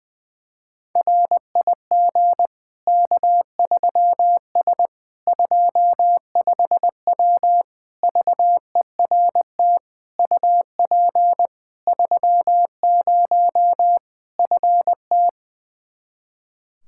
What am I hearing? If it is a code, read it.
Morse code, heard as a keyed tone, about 20 wpm, 700 Hz. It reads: RIG K3S 25W VERT UP 30 FT
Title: Practice QSO - Exchange 2 - Info - 20wpm